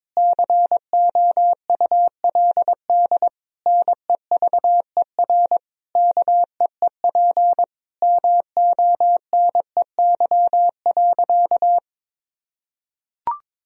Morse code, heard as a keyed tone, about 22 wpm, 700 Hz. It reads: COULD NEVER KEEP MONEY.
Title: Sentences from Top 1000 Words - Review Part 4 - 22wpm